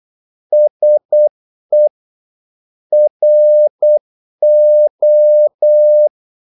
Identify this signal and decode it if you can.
Morse code, heard as a keyed tone, about 8 wpm, 600 Hz. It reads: SE RO
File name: Korte ord DK 8wpm